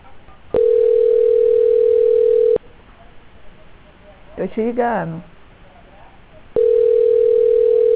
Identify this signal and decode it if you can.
Ringback tone (North America), 440 + 480 Hz, with speech.
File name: out- -2058- - - .44